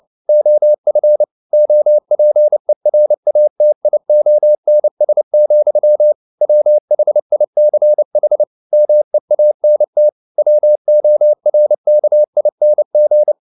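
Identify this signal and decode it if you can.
Morse code, heard as a keyed tone, about 29 wpm, 600 Hz. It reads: OF OPERATIONS, WHICH MEANT WORKING
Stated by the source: Morse code at 29 wpm, sending Ken Follett's novel Never in English, from CW med Gnister.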